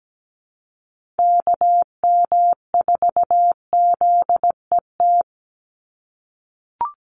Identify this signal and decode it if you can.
Morse code, heard as a keyed tone, about 17 wpm, 700 Hz. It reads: KM4ZET
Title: USA Call Signs 17wpm